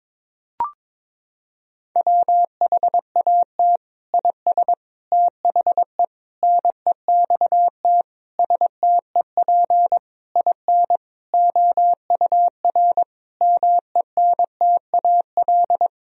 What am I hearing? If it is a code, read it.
Morse code, heard as a keyed tone, about 22 wpm, 700 Hz. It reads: WHAT IS THE NEXT STEP IN OUR MENTAL
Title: Sentences from Top 1000 Words - Review Part 4 - 22wpm